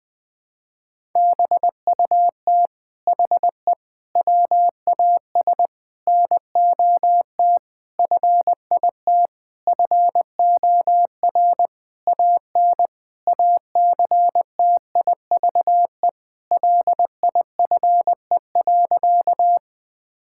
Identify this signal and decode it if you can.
Morse code, heard as a keyed tone, about 20 wpm, 700 Hz. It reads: BUT HE WAS NOT FIT FOR AN ACTIVE LIFE.
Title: Sentences from Top 1000 Words - Review Part 6 - 20wpm